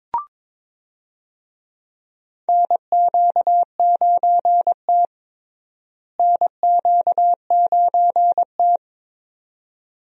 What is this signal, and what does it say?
Morse code, heard as a keyed tone, about 22 wpm, 700 Hz. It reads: NQ9T NQ9T
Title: USA Call Signs - 4 Characters - Repeated 2x - 22wpm